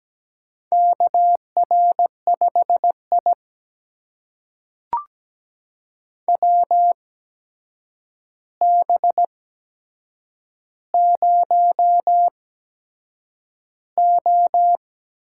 Morse code, heard as a keyed tone, about 17 wpm, spaced as for 7 wpm, 700 Hz. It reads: KR5I WB0O
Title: USA Call Signs - 4 Characters - 8x Character Spacing - 17wpm